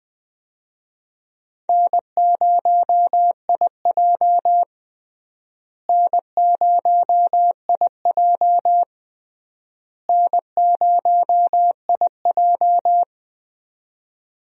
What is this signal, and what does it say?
Morse code, heard as a keyed tone, about 21 wpm, 700 Hz. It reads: N0IJ N0IJ N0IJ
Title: USA Call Signs - 4 Characters - Repeated 3x - 20wpm